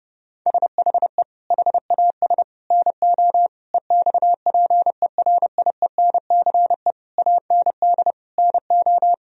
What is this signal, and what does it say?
Morse code, heard as a keyed tone, about 30 wpm, 700 Hz. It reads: SHE HAS NO EXPERIENCE AND NO
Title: Sentences from Top 1000 Words - Review Part 4 - 30wpm